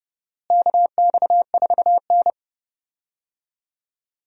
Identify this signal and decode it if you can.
Morse code, heard as a keyed tone, about 30 wpm, 700 Hz. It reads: KX4N
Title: USA Call Signs - 4 Characters - Repeated 2x - 30wpm